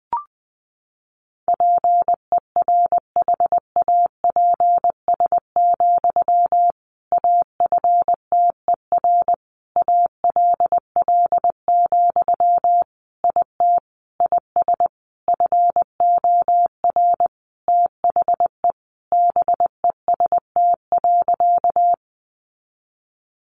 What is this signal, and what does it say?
Morse code, heard as a keyed tone, about 20 wpm, 700 Hz. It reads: PERHAPS, AFTER ALL, IT IS FOR THE BEST.
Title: Sentences from Top 500 Words 20wpm